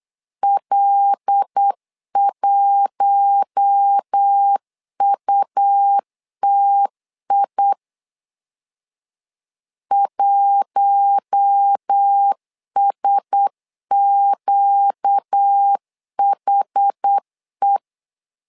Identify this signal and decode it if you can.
Morse code, heard as a keyed tone, about 8 wpm, 800 Hz. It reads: L1UTI 1SQHE
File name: LEZIONI / 5 LBVC / 5 L 8wpm 2831